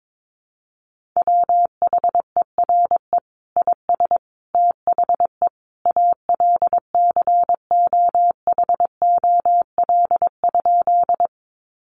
Morse code, heard as a keyed tone, about 22 wpm, 700 Hz. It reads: WHERE IS THE ALCOHOL?
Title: Sentences from Top 1400 Words 22wpm